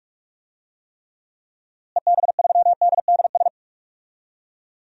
Morse code, heard as a keyed tone, about 46 wpm, 700 Hz. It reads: EB3DDS
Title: DXCC-100 Call Signs - 6 Characters - Repeated 2x - 45wpm